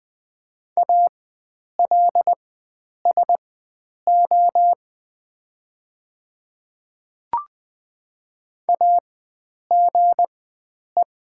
Morse code, heard as a keyed tone, about 20 wpm, spaced as for 9 wpm, 700 Hz. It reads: ALSO AGE
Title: 4x Character Spacing - Top 500 Words - 20wpm